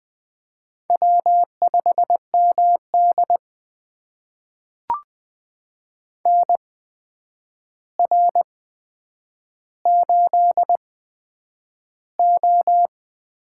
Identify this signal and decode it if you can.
Morse code, heard as a keyed tone, about 20 wpm, spaced as for 9 wpm, 700 Hz. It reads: W5MD NR8O
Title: USA Call Signs - 4 Characters - 8x Character Spacing - 20wpm